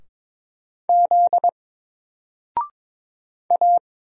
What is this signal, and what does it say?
Morse code, heard as a keyed tone, about 22 wpm, 700 Hz. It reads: Z A
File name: Single-Letter-Number-22wpm